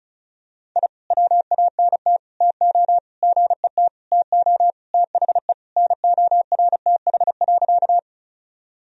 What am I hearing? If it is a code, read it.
Morse code, heard as a keyed tone, about 35 wpm, 700 Hz. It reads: I WANT TO GET TO THE NORTH.
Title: Sentences from Top 1000 Words - Review Part 1 - 35wpm